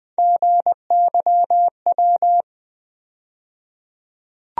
Morse code, heard as a keyed tone, about 20 wpm, 700 Hz. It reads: GYW
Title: USA Call Sign Postfixes - 3 Letters 20wpm